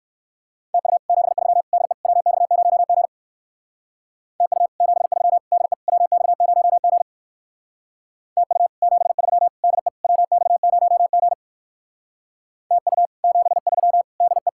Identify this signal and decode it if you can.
Morse code, heard as a keyed tone, about 64 wpm, 700 Hz. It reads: TU 73 DE WX0G TU 73 DE WX0G TU 73 DE WX0G TU 73 DE